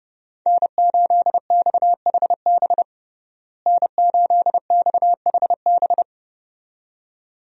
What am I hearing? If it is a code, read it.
Morse code, heard as a keyed tone, about 30 wpm, 700 Hz. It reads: N8XHB N8XHB
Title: USA Call Signs - 5 Characters - Repeated 2x - 30wpm